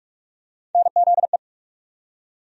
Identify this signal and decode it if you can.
Morse code, heard as a keyed tone, about 47 wpm, 700 Hz. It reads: NZE
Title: USA Call Sign Postfixes - 3 Letters 45wpm